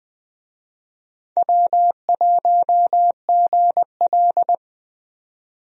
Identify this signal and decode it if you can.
Morse code, heard as a keyed tone, about 21 wpm, 700 Hz. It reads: W1GL
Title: USA Call Signs - 4 Characters - 20wpm